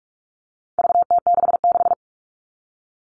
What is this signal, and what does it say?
Morse code, heard as a keyed tone, about 44 wpm, 700 Hz. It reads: VT66
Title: All Cut Numbers - 4-Digit 45wpm